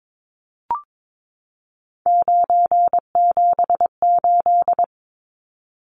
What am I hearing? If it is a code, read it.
Morse code, heard as a keyed tone, about 22 wpm, 700 Hz. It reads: 978
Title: Numbers - 3-Digit 22wpm